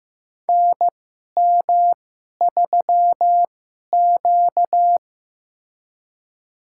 Morse code, heard as a keyed tone, about 15 wpm, spaced as for 11 wpm, 700 Hz. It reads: NM3Q
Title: USA Call Signs - 2x Character Spacing - 15wpm